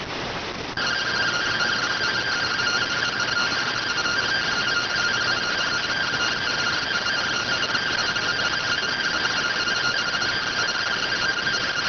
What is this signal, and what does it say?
Two-tone teleprinter signal, about 170 Hz shift.